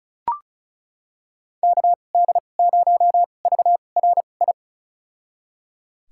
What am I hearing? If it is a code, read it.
Morse code, heard as a keyed tone, about 36 wpm, spaced as for 25 wpm, 700 Hz. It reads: KD0VRI
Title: USA Call Signs - 6 Characters - 2x Character Spacing - 35wpm